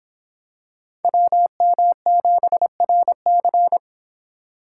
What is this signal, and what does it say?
Morse code, heard as a keyed tone, about 24 wpm, 700 Hz. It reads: WM7RC